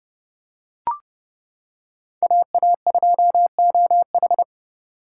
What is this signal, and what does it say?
Morse code, heard as a keyed tone, about 29 wpm, 700 Hz. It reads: AA2OH